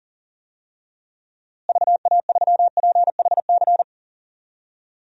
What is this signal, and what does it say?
Morse code, heard as a keyed tone, about 40 wpm, 700 Hz. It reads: VA3PHC
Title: DXCC-100 Call Signs - 6 Characters - Repeated 3x - 40wpm